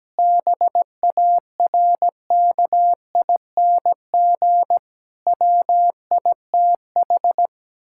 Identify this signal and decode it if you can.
Morse code, heard as a keyed tone, about 17 wpm, 700 Hz. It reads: BARKING WITH
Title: Pangram Challenge v2 - 17wpm